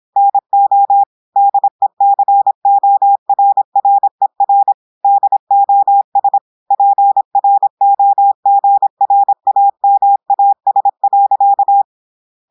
Morse code, heard as a keyed tone, about 26 wpm, 800 Hz. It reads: NO DECORRER DOS PROGRAMAS.
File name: portugais texte en clair 2 vitesse 26 mots minutes